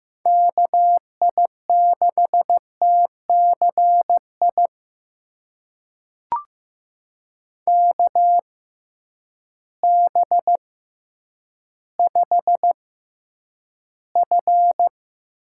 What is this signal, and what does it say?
Morse code, heard as a keyed tone, about 15 wpm, spaced as for 9 wpm, 700 Hz. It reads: KI6TCI KB5F
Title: USA Call Signs - 6x Character Spacing - 15wpm